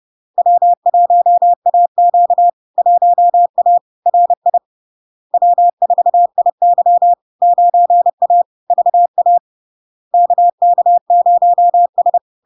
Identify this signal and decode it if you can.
Morse code, heard as a keyed tone, about 30 wpm, 700 Hz. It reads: W1AQ 1A RI W4IY 9A VA KK0S
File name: LARK-FD-2022-Log-in-CW-30wpm-700-Hz